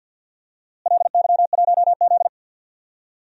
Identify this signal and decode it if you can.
Morse code, heard as a keyed tone, about 47 wpm, 700 Hz. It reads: LY1Z
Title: DXCC-100 Call Signs - 4 Characters - Repeated 2x - 50wpm